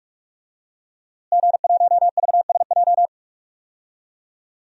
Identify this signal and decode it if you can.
Morse code, heard as a keyed tone, about 42 wpm, 700 Hz. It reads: G1VSJ
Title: DXCC-100 Call Signs - 5 Characters - Repeated 2x - 45wpm